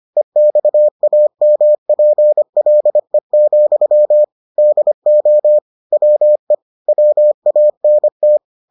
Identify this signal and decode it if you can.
Morse code, heard as a keyed tone, about 25 wpm, 600 Hz. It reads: EXAMPLE, DO WE WANT